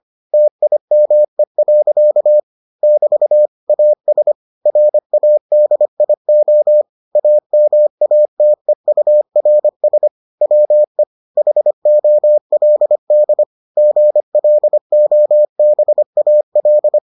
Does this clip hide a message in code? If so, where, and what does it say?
Yes: Morse code from 0.3 to 17.0 s, heard as a keyed tone, about 25 wpm, 600 Hz. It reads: TIME. <BT> AS RADIO AMATEURS WE HOLD GLOBAL